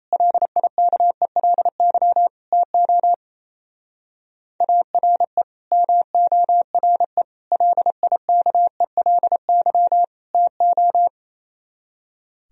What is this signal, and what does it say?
Morse code, heard as a keyed tone, about 35 wpm, 700 Hz. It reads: LIKELY TO ARE MORE LIKELY TO
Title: Speed-Racing - Sets of 4 Words 28wpm